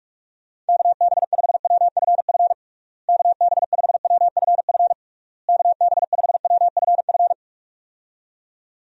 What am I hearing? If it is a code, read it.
Morse code, heard as a keyed tone, about 46 wpm, 700 Hz. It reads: KB5WFF KB5WFF KB5WFF
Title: USA Call Signs - 6 Characters - Repeated 3x - 45wpm